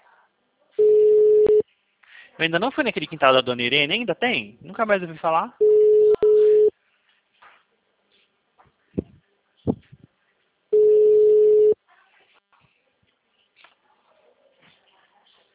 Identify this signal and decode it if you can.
Ringback tone (Europe, 425 Hz), with speech.